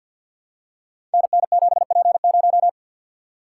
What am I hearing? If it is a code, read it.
Morse code, heard as a keyed tone, about 53 wpm, 700 Hz. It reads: NN7P0
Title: Code Groups - 5 Letter:Numbers - 50wpm